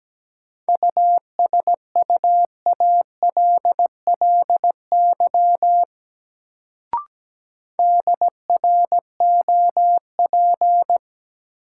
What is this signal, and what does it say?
Morse code, heard as a keyed tone, about 17 wpm, 700 Hz. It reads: USUALLY DROP
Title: Sentences from Top 1400 Words - New Word Drill Alt - 17wpm